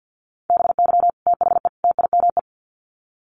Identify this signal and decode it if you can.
Morse code, heard as a keyed tone, about 50 wpm, 700 Hz. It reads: BY THE TIME
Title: Sets of 3 Words 50wpm